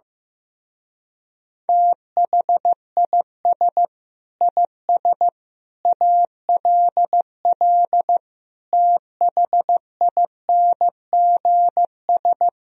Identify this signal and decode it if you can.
Morse code, heard as a keyed tone, about 15 wpm, 700 Hz. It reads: THIS IS ALL THINGS
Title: Speed-Racing - Sets of 5 Words 15wpm